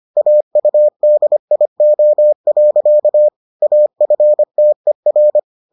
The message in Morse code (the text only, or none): AUDIO. AFTER